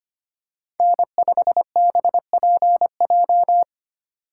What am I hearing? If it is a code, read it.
Morse code, heard as a keyed tone, about 25 wpm, 700 Hz. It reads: N5BPJ
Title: USA Call Signs - 5 Characters - 3x Character Spacing - 25wpm